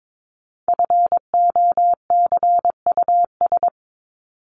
Morse code, heard as a keyed tone, about 22 wpm, 700 Hz. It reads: FOCUS
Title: Top 501-750 Words 22wpm